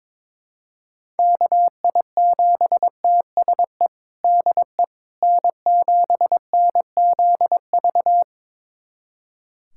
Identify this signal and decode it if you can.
Morse code, heard as a keyed tone, about 22 wpm, 700 Hz. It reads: KI7TSE DE N7NZV
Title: USA Callsign DE Callsign 22wpm